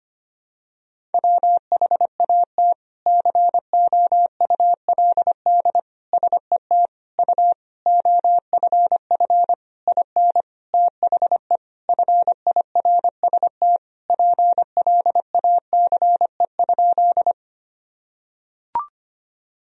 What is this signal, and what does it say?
Morse code, heard as a keyed tone, about 25 wpm, 700 Hz. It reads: WHAT COULD SET U OFF IN THE FIRST PLACE?
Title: Sentences from Top 200 Words and Abbreviations 25wpm